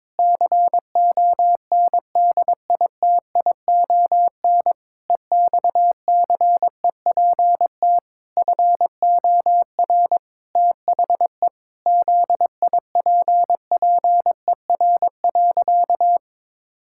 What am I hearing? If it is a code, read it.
Morse code, heard as a keyed tone, about 22 wpm, 700 Hz. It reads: CONDITION EXCEPT FOR THE ZIPPER.
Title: Pangram Challenge v2 - 22wpm